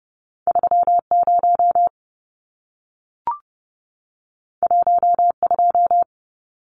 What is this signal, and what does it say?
Morse code, heard as a keyed tone, about 30 wpm, 700 Hz. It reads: 30 12
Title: Numbers - 2-Digit 30wpm